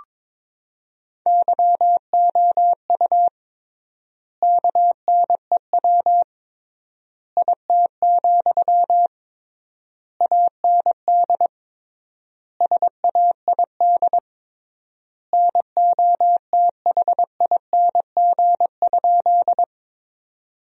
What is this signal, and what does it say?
Morse code, heard as a keyed tone, about 22 wpm, 700 Hz. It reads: YOU KNEW IT, AND SAID NOTHING?
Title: 3x Word-Spacing - Sentences from Top 500 Words - 22wpm